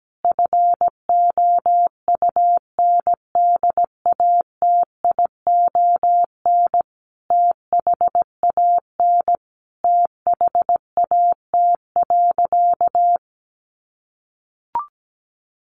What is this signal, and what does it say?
Morse code, heard as a keyed tone, about 17 wpm, 700 Hz. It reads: FOUNDATION THAN THAT.
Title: Sentences from Top 1200 Words 17wpm